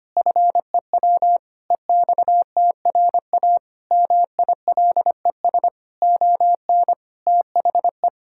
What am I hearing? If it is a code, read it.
Morse code, heard as a keyed tone, about 25 wpm, 700 Hz. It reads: FEW EXTRA MILES ON THE
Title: Pangram Challenge v2 - 25wpm